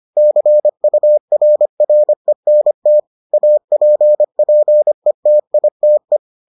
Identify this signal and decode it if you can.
Morse code, heard as a keyed tone, about 25 wpm, 600 Hz. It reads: CURRENT APPETITE